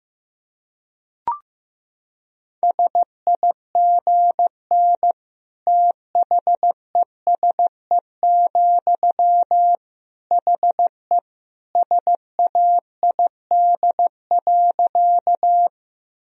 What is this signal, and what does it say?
Morse code, heard as a keyed tone, about 15 wpm, 700 Hz. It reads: SIGN THESE, HE SAID.
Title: Sentences from Top 1000 Words - Review Part 1 - 15wpm